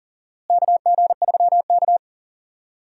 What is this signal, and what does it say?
Morse code, heard as a keyed tone, about 39 wpm, 700 Hz. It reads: KG3K
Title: USA Call Signs - 4 Characters - Repeated 2x - 40wpm